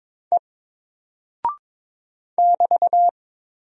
Morse code, heard as a keyed tone, about 22 wpm, 700 Hz. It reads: E <BT>
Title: Single Character (Letter:Number:Punctuation:Prosign) 22wpm